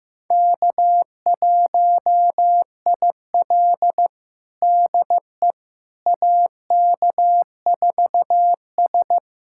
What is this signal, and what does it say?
Morse code, heard as a keyed tone, about 15 wpm, 700 Hz. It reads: K1IL DE AK4S
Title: USA Callsign DE Callsign - 4 Characters - 15wpm